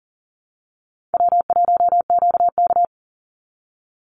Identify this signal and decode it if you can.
Morse code, heard as a keyed tone, about 40 wpm, 700 Hz. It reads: W1QK
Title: Easy Call Signs Repeated 3x - 40wpm